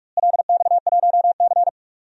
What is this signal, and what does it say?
Morse code, heard as a keyed tone, about 41 wpm, 700 Hz. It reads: LX1C
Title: DXCC-100 Call Signs - 4 Characters - Repeated 2x - 45wpm